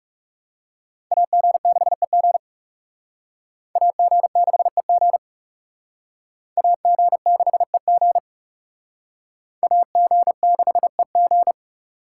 Morse code, heard as a keyed tone, about 37 wpm, 700 Hz. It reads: AG6EG AG6EG AG6EG AG6EG
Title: Speed Racing - USA Call Signs 30wpm